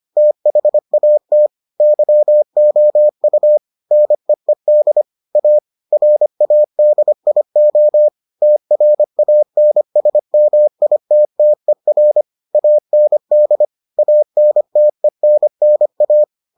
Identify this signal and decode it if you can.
Morse code, heard as a keyed tone, about 25 wpm, 600 Hz. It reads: THAT YOU NEED A RADIO TRANSMITTER AND ANTENNA